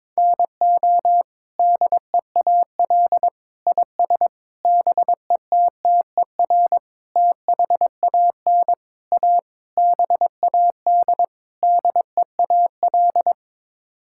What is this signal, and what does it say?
Morse code, heard as a keyed tone, about 22 wpm, 700 Hz. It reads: NO DEAL IS BETTER THAN A BAD DEAL
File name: practice / 22wpm / Sets of 8 Words - 22wpm